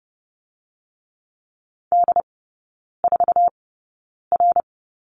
Morse code, heard as a keyed tone, about 30 wpm, spaced as for 9 wpm, 700 Hz. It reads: D4R
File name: Lesson 16e - Letter c - Three Characters 30wpm